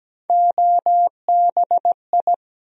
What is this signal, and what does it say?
Morse code, heard as a keyed tone, about 17 wpm, 700 Hz. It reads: OBI
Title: USA Call Sign Postfixes - 3 Letters 17wpm